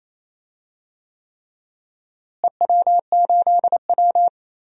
Morse code, heard as a keyed tone, about 28 wpm, 700 Hz. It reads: EW8W
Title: DXCC-100 Call Signs - 4 Characters - Repeated 3x - 28wpm